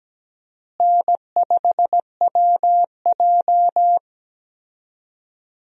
Morse code, heard as a keyed tone, about 17 wpm, 700 Hz. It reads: N5WJ
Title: USA Call Signs - 4 Characters - Repeated 2x - 17wpm